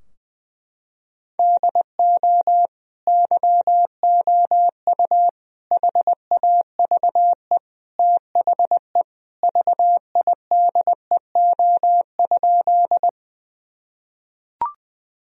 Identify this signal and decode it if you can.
Morse code, heard as a keyed tone, about 21 wpm, 700 Hz. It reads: DO YOU HAVE THE VIDEO?
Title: Sentences from Top 1000 Words - Review Part 4 - 20wpm